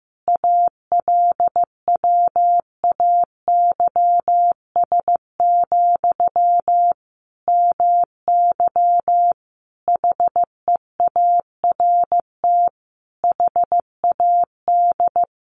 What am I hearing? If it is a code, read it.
Morse code, heard as a keyed tone, about 15 wpm, 700 Hz. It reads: ALWAYS, MY HEART HAD
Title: Sentences from Top 1200 Words 15wpm